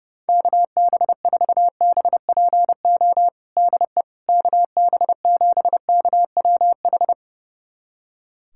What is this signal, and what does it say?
Morse code, heard as a keyed tone, about 32 wpm, 700 Hz. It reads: KB4BPO DE KB7KWH
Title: USA Callsign DE Callsign - 6 Characters - 30wpm